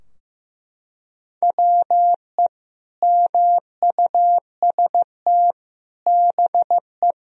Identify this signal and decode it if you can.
Morse code, heard as a keyed tone, about 16 wpm, 700 Hz. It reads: WE MUST BE
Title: Sentences from Top 1000 Words - Review Part 4 - 15wpm